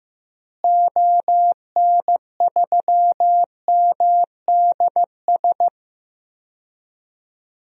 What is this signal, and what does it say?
Morse code, heard as a keyed tone, about 15 wpm, 700 Hz. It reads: ON3MDS
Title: DXCC-100 Call Signs - 6 Characters - Repeated 2x - 15wpm